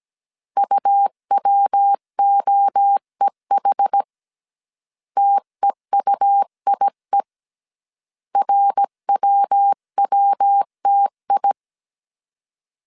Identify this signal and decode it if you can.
Morse code, heard as a keyed tone, about 17 wpm, 800 Hz. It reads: UWOEH TEUIE RWWTI